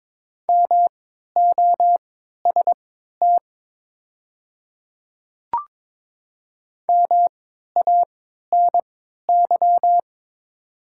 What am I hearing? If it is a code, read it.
Morse code, heard as a keyed tone, about 22 wpm, spaced as for 12 wpm, 700 Hz. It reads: MOST MANY
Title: 3x Character Spacing - Top 100 Words - 22wpm